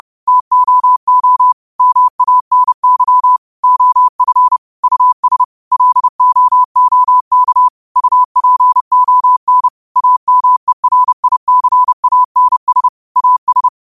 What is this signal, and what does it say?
Morse code, heard as a keyed tone, about 30 wpm, 1000 Hz. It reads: TOO MANY OF US LOOK UPON AMERICANS AS
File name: qotdfiles / 30wpm